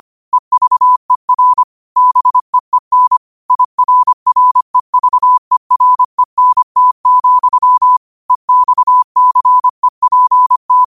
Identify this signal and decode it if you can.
Morse code, heard as a keyed tone, about 25 wpm, 1000 Hz. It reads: EVER BEEN IRREVERENT, EXCEPT
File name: qotdfiles / 25wpm